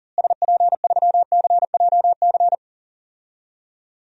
Morse code, heard as a keyed tone, about 40 wpm, 700 Hz. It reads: SP3CJC
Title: DXCC-100 Call Signs - 6 Characters - Repeated 3x - 40wpm